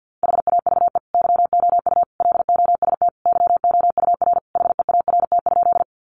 Morse code, heard as a keyed tone, about 50 wpm, 700 Hz. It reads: HAVE YOU LOST YOUR HEART?
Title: Sentences from Top 1400 Words 50wpm